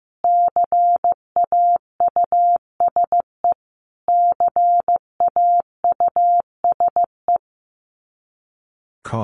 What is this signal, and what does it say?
Morse code, heard as a keyed tone, about 15 wpm, 700 Hz. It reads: CAUSE CAUSE
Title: Top 200 Words Repeated 15wpm